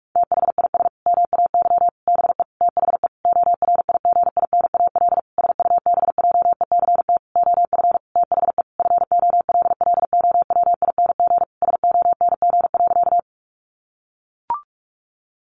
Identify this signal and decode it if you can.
Morse code, heard as a keyed tone, about 44 wpm, 700 Hz. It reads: THIS MAY BE THE ORIGINAL SUBJECT OF THE FOLLOWING SONG.
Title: Sentences from Top 900 Words 45wpm